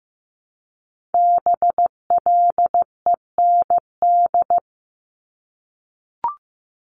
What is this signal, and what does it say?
Morse code, heard as a keyed tone, about 15 wpm, 700 Hz. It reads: BLEND
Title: Multi-letter Phonemes Repeated 3x 15wpm